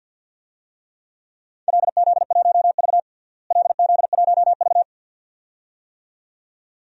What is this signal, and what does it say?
Morse code, heard as a keyed tone, about 52 wpm, 700 Hz. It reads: LZ1V LZ1V
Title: DXCC-100 Call Signs - 4 Characters - Repeated 2x - 50wpm